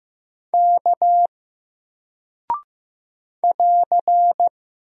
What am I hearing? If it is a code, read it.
Morse code, heard as a keyed tone, about 15 wpm, 700 Hz. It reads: K <AR>
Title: Single Character (Letter:Number:Punctuation:Prosign) Alt 15wpm